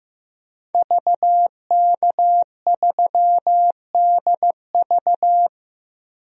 Morse code, heard as a keyed tone, about 15 wpm, 700 Hz. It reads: VK3DV
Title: DXCC-100 Call Signs - 5 Characters - Repeated 2x - 15wpm